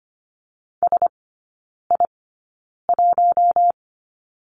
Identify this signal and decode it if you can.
Morse code, heard as a keyed tone, about 25 wpm, spaced as for 9 wpm, 690 Hz. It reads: SI1
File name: Lesson 14e - Number 2 - Three Characters 25wpm